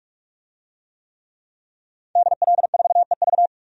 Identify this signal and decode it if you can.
Morse code, heard as a keyed tone, about 46 wpm, 700 Hz. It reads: DL4EV
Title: DXCC-100 Call Signs - 5 Characters - Repeated 3x - 45wpm